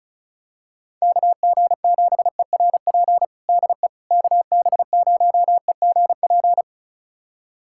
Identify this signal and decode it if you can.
Morse code, heard as a keyed tone, about 34 wpm, 690 Hz. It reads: KG7ERP DE KB0EGP
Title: USA Callsign DE Callsign 35wpm